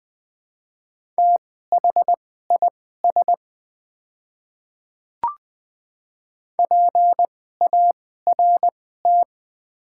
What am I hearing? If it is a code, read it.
Morse code, heard as a keyed tone, about 20 wpm, spaced as for 15 wpm, 700 Hz. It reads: THIS PART
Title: 2x Character Spacing - Top 300 Words - 20wpm